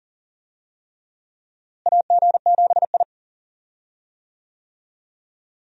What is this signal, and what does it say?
Morse code, heard as a keyed tone, about 37 wpm, 700 Hz. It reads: AG7I
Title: USA Call Signs - 4 Characters - Repeated 3x - 40wpm